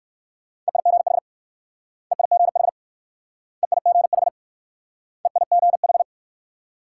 Morse code, heard as a keyed tone, about 56 wpm, 700 Hz. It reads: EIGH EIGH EIGH EIGH
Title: Speed-Racing Multi-letter Phonemes - 45wpm